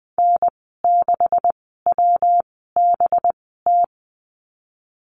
Morse code, heard as a keyed tone, about 20 wpm, spaced as for 15 wpm, 700 Hz. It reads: N6WBT
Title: 193 Lesson 31g - Letter v - Call Signs 20wpm